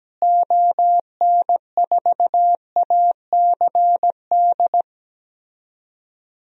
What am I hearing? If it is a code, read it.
Morse code, heard as a keyed tone, about 17 wpm, 700 Hz. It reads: ON4ACD
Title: DXCC-100 Call Signs - 6 Characters - Repeated 2x - 17wpm